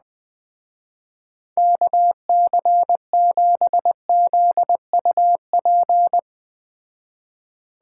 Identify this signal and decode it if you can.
Morse code, heard as a keyed tone, about 20 wpm, 700 Hz. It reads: KC7ZUP